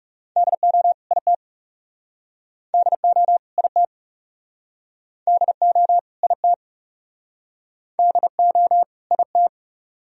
Morse code, heard as a keyed tone, about 39 wpm, 700 Hz. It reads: DO IT DO IT DO IT DO IT